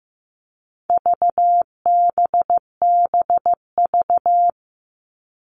Morse code, heard as a keyed tone, about 15 wpm, 700 Hz. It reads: VBBV
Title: All Cut Numbers - 4-Digit 15wpm